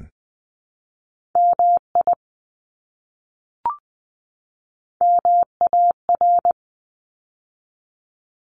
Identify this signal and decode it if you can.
Morse code, heard as a keyed tone, about 20 wpm, 700 Hz. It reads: MI MAR
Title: ARRL Sections 20wpm